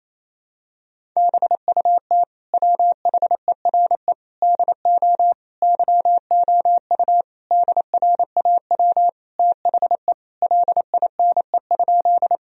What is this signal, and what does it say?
Morse code, heard as a keyed tone, about 28 wpm, 700 Hz. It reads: BUT WHERE DO YOU DRAW THE LINE?
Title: Sentences from Top 1000 Words - Review Part 1 - 28wpm